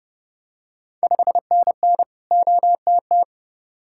Morse code, heard as a keyed tone, about 30 wpm, 700 Hz. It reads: 5NN OTT
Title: Ontario QSO Party Exchange 30wpm